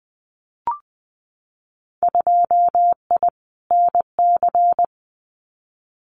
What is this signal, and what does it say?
Morse code, heard as a keyed tone, about 20 wpm, 700 Hz. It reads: 2I NC
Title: Winter Field Day Exchanges 20wpm